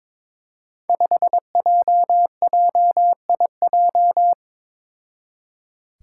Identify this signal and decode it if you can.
Morse code, heard as a keyed tone, about 22 wpm, 700 Hz. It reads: 5JJIJ
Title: Five Challenging Letter-Number Combinations 22wpm